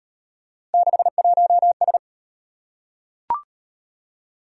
Morse code, heard as a keyed tone, about 32 wpm, 700 Hz. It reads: 61S